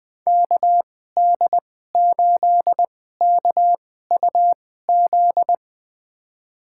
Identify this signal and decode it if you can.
Morse code, heard as a keyed tone, about 21 wpm, spaced as for 15 wpm, 700 Hz. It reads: KD8KUZ
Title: USA Call Signs - 6 Characters - 2x Character Spacing - 20wpm